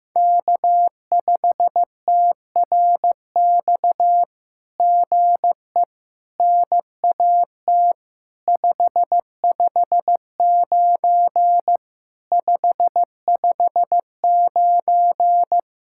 Morse code, heard as a keyed tone, about 14 wpm, 700 Hz. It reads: K5TRX GE NAT 559 559
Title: SOTA QSO Practice 15wpm